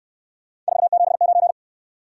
Morse code, heard as a keyed tone, about 68 wpm, 700 Hz. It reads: 479